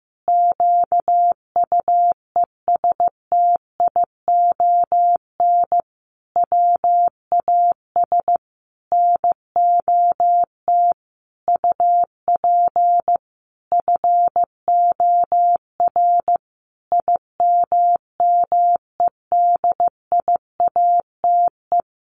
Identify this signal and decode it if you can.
Morse code, heard as a keyed tone, about 15 wpm, 700 Hz. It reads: QUESTION WAS NOT UP FOR IMMEDIATE
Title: Sentences from Top 1200 Words 15wpm